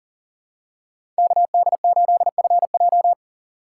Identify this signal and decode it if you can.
Morse code, heard as a keyed tone, about 40 wpm, 700 Hz. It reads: KD8FJ
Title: USA Call Signs - 5 Characters - 6x Character Spacing - 40wpm